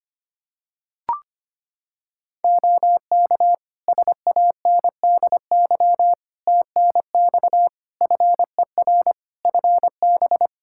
Morse code, heard as a keyed tone, about 25 wpm, 700 Hz. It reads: OK SANDY TNX FER FB
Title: K1USN SST QSO Practice 25wpm